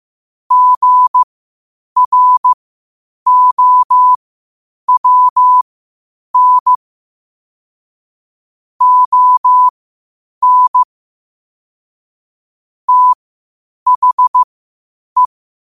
Morse code, heard as a keyed tone, about 15 wpm, spaced as for 8 wpm, 1000 Hz. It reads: GROWN ON THE
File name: qotdfiles / 8wpm